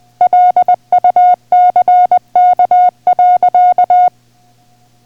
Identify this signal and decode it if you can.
Morse code, heard as a keyed tone, about 20 wpm, 720 Hz. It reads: LUCK.